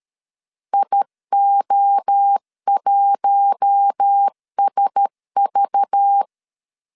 Morse code, heard as a keyed tone, about 13 wpm, 800 Hz. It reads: IO1SV
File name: CW / DET / 1 6 1500C S C 12wpm 3533